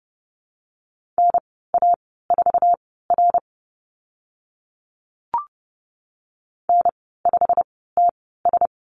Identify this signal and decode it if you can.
Morse code, heard as a keyed tone, about 30 wpm, spaced as for 17 wpm, 700 Hz. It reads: NA4R N5TS
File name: Lesson 15g - Number 5 - Call Signs 30wpm